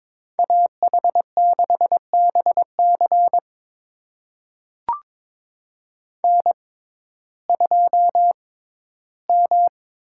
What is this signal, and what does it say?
Morse code, heard as a keyed tone, about 21 wpm, spaced as for 13 wpm, 700 Hz. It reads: AH6BC N2M
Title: USA Call Signs - 5 Characters - 6x Character Spacing - 22wpm